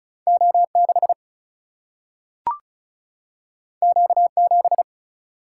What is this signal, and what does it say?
Morse code, heard as a keyed tone, about 36 wpm, 700 Hz. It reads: O 6 Q 7
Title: Single-Letter Single-Number 35wpm